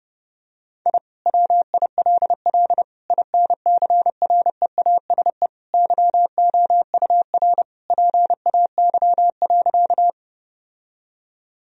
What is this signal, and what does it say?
Morse code, heard as a keyed tone, about 30 wpm, 700 Hz. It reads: I WILL INCREASE YOUR PAY.
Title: Sentences from Top 1000 Words - Review Part 4 - 30wpm